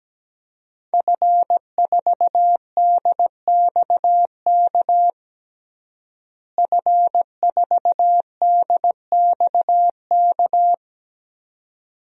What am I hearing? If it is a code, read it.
Morse code, heard as a keyed tone, about 17 wpm, 700 Hz. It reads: F4DXK F4DXK
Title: DXCC-100 Call Signs - 5 Characters - Repeated 2x - 17wpm